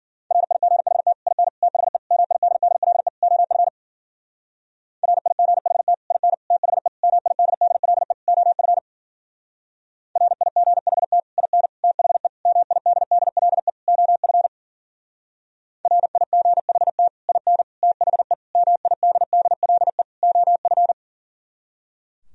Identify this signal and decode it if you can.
Morse code, heard as a keyed tone, about 51 wpm, 700 Hz. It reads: RIGHT IN THE MIDDLE OF RIGHT IN THE MIDDLE OF RIGHT IN THE MIDDLE OF RIGHT IN THE MIDDLE OF